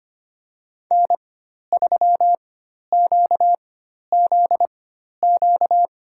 Morse code, heard as a keyed tone, about 25 wpm, spaced as for 12 wpm, 700 Hz. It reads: N3QZQ
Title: USA Call Signs - 5 Characters - 4x Character Spacing - 25wpm